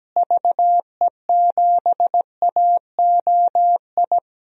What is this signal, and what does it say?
Morse code, heard as a keyed tone, about 17 wpm, 700 Hz. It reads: VE7AOI
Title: DXCC-100 Call Signs - 6 Characters - Repeated 3x - 17wpm